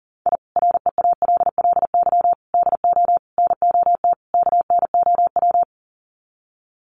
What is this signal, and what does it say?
Morse code, heard as a keyed tone, about 37 wpm, 700 Hz. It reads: I REALLY DO NOT KNOW
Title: Sets of 5 Words Encore 40wpm